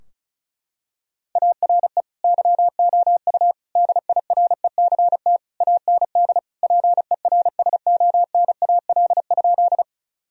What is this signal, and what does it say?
Morse code, heard as a keyed tone, about 36 wpm, 700 Hz. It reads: ARE YOU DIRECT AND PERSONAL?
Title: Sentences from Top 1000 Words - Review Part 4 - 35wpm